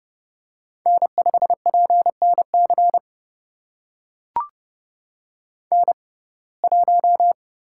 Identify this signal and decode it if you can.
Morse code, heard as a keyed tone, about 30 wpm, spaced as for 22 wpm, 700 Hz. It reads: N5PNC N1
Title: USA Call Signs - 5 Characters - 6x Character Spacing - 30wpm